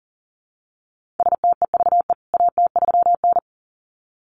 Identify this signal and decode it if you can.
Morse code, heard as a keyed tone, about 40 wpm, 700 Hz. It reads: STEVE AT3N
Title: CWT Exchange 40wpm